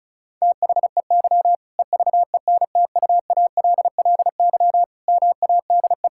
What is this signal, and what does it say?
Morse code, heard as a keyed tone, about 36 wpm, 700 Hz. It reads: THEY EVENTUALLY MADE
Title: Sentences from Top 1000 Words - Review Part 6 - 35wpm